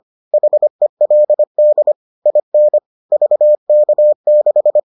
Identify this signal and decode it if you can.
Morse code, heard as a keyed tone, about 25 wpm, 600 Hz. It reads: HELD IN VK6